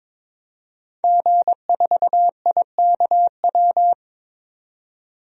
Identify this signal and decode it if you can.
Morse code, heard as a keyed tone, about 22 wpm, 700 Hz. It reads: G4IKW
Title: DXCC-100 Call Signs - 5 Characters - 22wpm